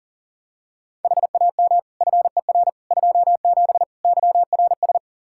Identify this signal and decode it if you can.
Morse code, heard as a keyed tone, about 40 wpm, 700 Hz. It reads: HAM FER 27 YRS
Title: Practice QSO - Exchange 3 - More Info - 40wpm